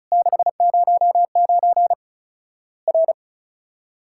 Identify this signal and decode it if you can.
Morse code, heard as a keyed tone, about 34 wpm, 700 Hz. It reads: 609 R
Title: Worked All Europe DX Contest - QTC Exchange - 35wpm